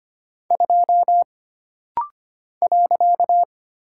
Morse code, heard as a keyed tone, about 25 wpm, 700 Hz. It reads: 2 .
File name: Single Character (Letter:Number:Punctuation:Prosign) - Rapid-Fire - 25wpm